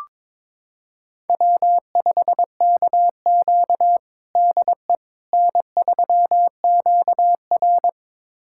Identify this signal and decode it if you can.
Morse code, heard as a keyed tone, about 22 wpm, 700 Hz. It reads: W5KQ DE N3QR
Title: USA Callsign DE Callsign - 4 Characters - 22wpm